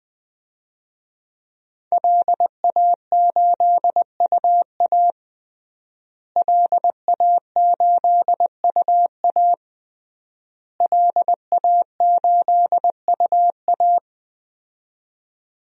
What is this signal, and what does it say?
Morse code, heard as a keyed tone, about 19 wpm, 700 Hz. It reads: LA8UA LA8UA LA8UA
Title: DXCC-100 Call Signs - 5 Characters - Repeated 3x - 20wpm